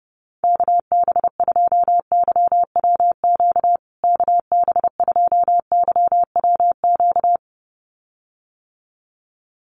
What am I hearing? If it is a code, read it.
Morse code, heard as a keyed tone, about 30 wpm, 700 Hz. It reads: KB2YWQ KB2YWQ
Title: USA Call Signs Repeated 2x 30wpm